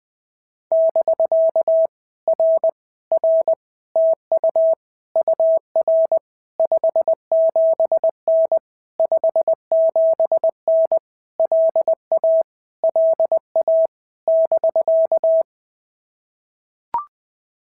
Morse code, heard as a keyed tone, about 19 wpm, 650 Hz. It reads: <BK> R R TU UR 57N 57N LA LA <BK>
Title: POTA QSO Practice - 20wpm